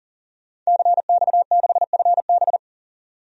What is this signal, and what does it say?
Morse code, heard as a keyed tone, about 37 wpm, 700 Hz. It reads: CX6FB
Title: DXCC-100 Call Signs - 5 Characters - 40wpm